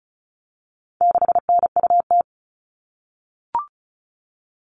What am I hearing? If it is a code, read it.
Morse code, heard as a keyed tone, about 34 wpm, 700 Hz. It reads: 6NUT
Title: All Cut Numbers - 4-Digit 35wpm